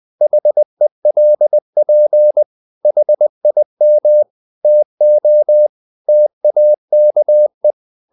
Morse code, heard as a keyed tone, about 20 wpm, 600 Hz. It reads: HELP HIM TO TAKE
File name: the old man and the sea / the old man and the sea part 1 20wpm